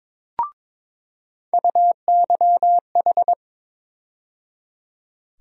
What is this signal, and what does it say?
Morse code, heard as a keyed tone, about 22 wpm, 700 Hz. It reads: UYH
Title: USA Call Sign Postfixes - 3 Letters 22wpm